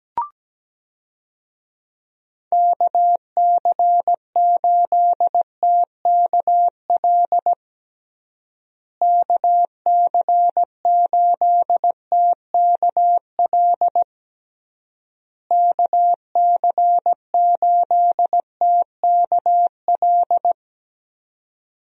Morse code, heard as a keyed tone, about 17 wpm, 700 Hz. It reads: KC8TKL KC8TKL KC8TKL
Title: USA Call Signs - 6 Characters - Repeated 3x - 17wpm